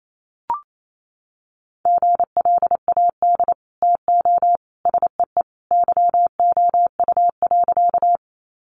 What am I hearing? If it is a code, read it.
Morse code, heard as a keyed tone, about 28 wpm, 700 Hz. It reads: GLAD TO SEE YOU.
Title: Sentences from Top 800 Words 28wpm